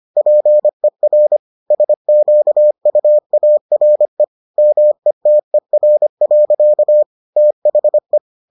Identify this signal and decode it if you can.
Morse code, heard as a keyed tone, about 25 wpm, 600 Hz. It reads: PER SQUARE METER. THE